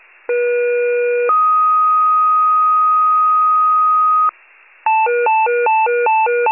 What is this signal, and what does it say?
Two-tone page, heard as a one-second tone, then a three-second tone, about 480 then 1250 Hz. It is heard from 0.3 to 4.3 s.
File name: wcalert